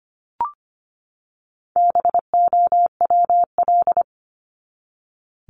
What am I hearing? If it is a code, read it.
Morse code, heard as a keyed tone, about 25 wpm, 700 Hz. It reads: BOWL
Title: Top 1001-1250 Words 25wpm